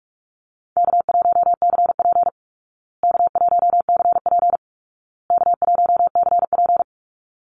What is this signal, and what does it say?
Morse code, heard as a keyed tone, about 44 wpm, 700 Hz. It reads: K1CP K1CP K1CP
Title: Easy Call Signs Repeated 3x - 45wpm